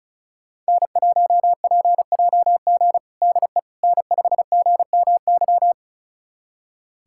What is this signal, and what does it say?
Morse code, heard as a keyed tone, about 34 wpm, 700 Hz. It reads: N1PJG DE N5GMY
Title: USA Callsign DE Callsign - 5 Characters - 35wpm